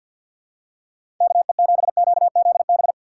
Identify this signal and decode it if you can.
Morse code, heard as a keyed tone, about 52 wpm, 700 Hz. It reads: KE7QZB
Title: USA Call Signs - 3x Character Spacing - 50wpm